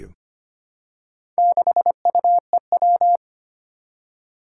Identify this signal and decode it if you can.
Morse code, heard as a keyed tone, about 25 wpm, 700 Hz. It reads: 6UEW
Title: Four Challenging Letter-Number Combinations 25wpm